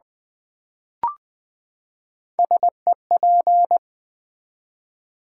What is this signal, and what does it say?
Morse code, heard as a keyed tone, about 20 wpm, 700 Hz. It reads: SEP
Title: Build A Word - 3 Letters At A Time - 9-Letter Words - 20wpm